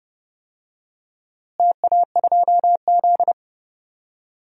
Morse code, heard as a keyed tone, about 30 wpm, 700 Hz. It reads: TA2Z
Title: DXCC-100 Call Signs - 4 Characters - Repeated 3x - 30wpm